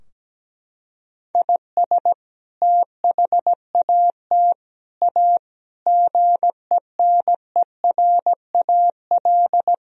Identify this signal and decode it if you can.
Morse code, heard as a keyed tone, about 17 wpm, 700 Hz. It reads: IS THAT A GENERAL
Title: Sentences from Top 1000 Words - Review Part 3 - 17wpm